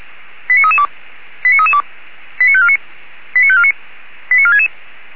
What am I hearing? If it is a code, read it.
Five-tone selective call (ZVEI), heard as quick runs of five tones, tones of about 70 ms. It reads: 89332 89332 89549 89549 89466
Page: audio.f85360.feuerwehr-orlamuende.org